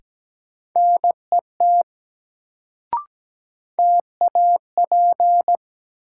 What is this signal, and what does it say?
Morse code, heard as a keyed tone, about 17 wpm, 700 Hz. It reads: NET TAP
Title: Sentences from Top 1400 Words - New Word Drill Alt - 17wpm